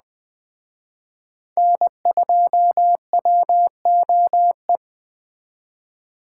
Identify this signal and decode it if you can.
Morse code, heard as a keyed tone, about 20 wpm, 700 Hz. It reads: N2WOE
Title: Speed Racing - USA Call Signs 17wpm